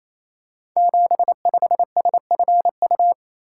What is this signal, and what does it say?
Morse code, heard as a keyed tone, about 27 wpm, 700 Hz. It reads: 75SFU
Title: Code Groups - 5 Letter:Numbers - 28wpm